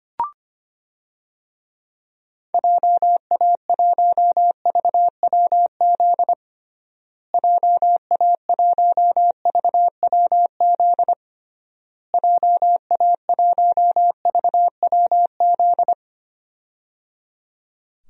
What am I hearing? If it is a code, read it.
Morse code, heard as a keyed tone, about 25 wpm, 700 Hz. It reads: JA1VWZ JA1VWZ JA1VWZ
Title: DXCC-100 Call Signs - 6 Characters - Repeated 3x - 25wpm